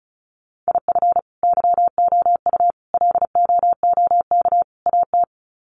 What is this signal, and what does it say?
Morse code, heard as a keyed tone, about 34 wpm, 700 Hz. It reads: IF YOU LOOK AT
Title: Sets of 4 Words 35wpm